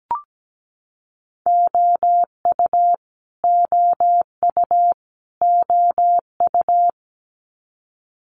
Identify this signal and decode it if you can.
Morse code, heard as a keyed tone, about 17 wpm, 700 Hz. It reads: OU OU OU
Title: Multi-letter Phonemes Repeated 3x 17wpm